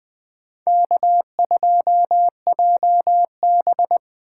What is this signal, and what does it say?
Morse code, heard as a keyed tone, about 20 wpm, 700 Hz. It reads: K2JB
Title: 2023 SOTA Chaser Call Signs - 20wpm